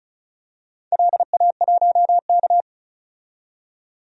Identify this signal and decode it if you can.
Morse code, heard as a keyed tone, about 36 wpm, 700 Hz. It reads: LA1K
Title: DXCC-100 Call Signs - 4 Characters - 35wpm